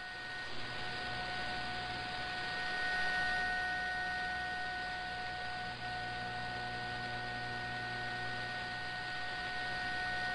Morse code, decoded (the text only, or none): none